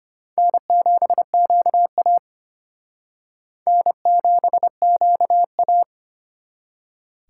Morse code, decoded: N7QA N7QA

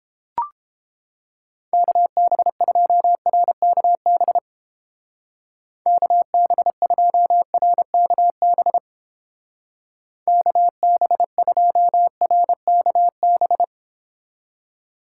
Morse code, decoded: KB2RKB KB2RKB KB2RKB

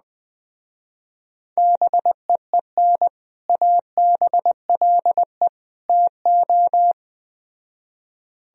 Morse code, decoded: BEEN ABLE TO

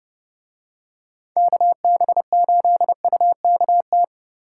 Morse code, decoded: KB8UKT

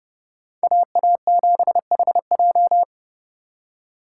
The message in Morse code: AA7HJ

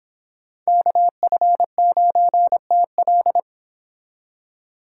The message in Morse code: KF9TL

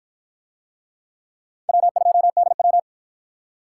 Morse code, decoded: W2DW

